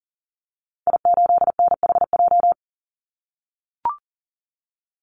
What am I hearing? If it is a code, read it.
Morse code, heard as a keyed tone, about 40 wpm, 700 Hz. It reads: I8NHJ
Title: Call Signs - 40wpm